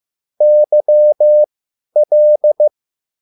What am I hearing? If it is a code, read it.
Morse code, heard as a keyed tone, about 15 wpm, spaced as for 10 wpm, 600 Hz. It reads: YL
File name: QSL / CW lesson